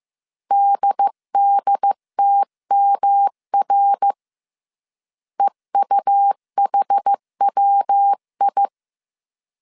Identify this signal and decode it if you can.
Morse code, heard as a keyed tone, about 15 wpm, 800 Hz. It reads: DDTMR EUHWI
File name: CW / DET / 1 3 1500C S A 14wpm 2447